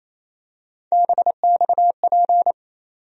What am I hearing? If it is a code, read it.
Morse code, heard as a keyed tone, about 27 wpm, 700 Hz. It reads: BXP
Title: USA Call Sign Postfixes - 3 Letters 28wpm